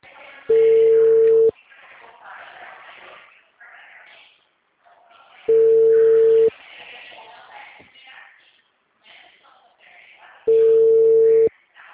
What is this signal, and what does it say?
Ringback tone (China, 450 Hz), with speech.
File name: out- -2045- - - .10886